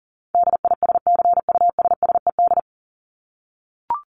Morse code, heard as a keyed tone, about 40 wpm, 700 Hz. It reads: DISCUSSED
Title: ED Suffix - Top 100 Words with Base Words 40wpm